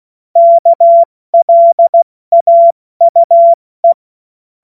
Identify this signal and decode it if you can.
Morse code, heard as a keyed tone, about 16 wpm, spaced as for 14 wpm, 680 Hz. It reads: KLAUE
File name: Morsekurs / Witze / Witz 28.November 2023 70 F80